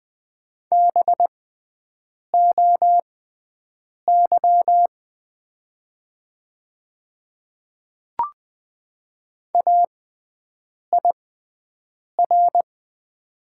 Morse code, heard as a keyed tone, about 20 wpm, spaced as for 7 wpm, 700 Hz. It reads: BOY AIR
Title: 6x Character Spacing - Top 300 Words - 20wpm